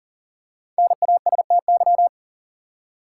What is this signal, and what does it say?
Morse code, heard as a keyed tone, about 37 wpm, 700 Hz. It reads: NASTY
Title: Five Letter Words - 40wpm